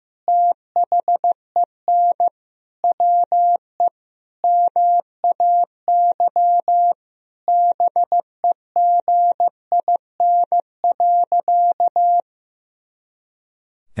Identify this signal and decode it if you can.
Morse code, heard as a keyed tone, about 16 wpm, 700 Hz. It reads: THEN WE MAY BEGIN.
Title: Sentences from Top 300 Words and Abbreviations 15wpm